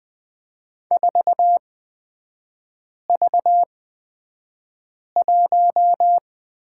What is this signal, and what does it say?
Morse code, heard as a keyed tone, about 20 wpm, spaced as for 5 wpm, 700 Hz. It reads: 4V1
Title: 250 Lesson 40e - BK - Three Characters 20wpm